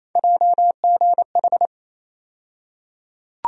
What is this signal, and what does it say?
Morse code, heard as a keyed tone, about 28 wpm, 700 Hz. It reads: JGH
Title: USA Call Sign Postfixes - 3 Letters 28wpm